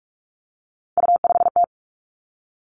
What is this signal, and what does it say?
Morse code, heard as a keyed tone, about 46 wpm, 700 Hz. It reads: U5T
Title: Three Challenging Letter-Number Combinations 45wpm